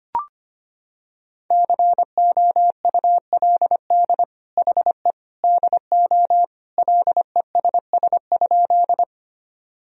Morse code, heard as a keyed tone, about 25 wpm, 700 Hz. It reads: COULD HE DO LESS?
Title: Sentences from Top 1000 Words - Review Part 2 - 25wpm